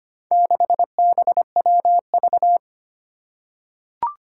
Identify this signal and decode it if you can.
Morse code, heard as a keyed tone, about 25 wpm, 700 Hz. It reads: 6BWV
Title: Four Challenging Letter-Number Combinations 25wpm